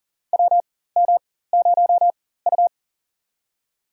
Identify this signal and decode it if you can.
Morse code, heard as a keyed tone, about 41 wpm, spaced as for 19 wpm, 700 Hz. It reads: WM0U
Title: USA Call Signs - 4 Characters - 4x Character Spacing - 40wpm